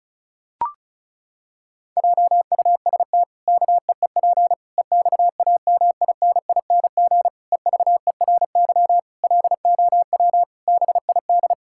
Morse code, heard as a keyed tone, about 36 wpm, 700 Hz. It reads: JUST KEEP EXAMINING EVERY LOW BID
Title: Pangram Challenge v2 - 35wpm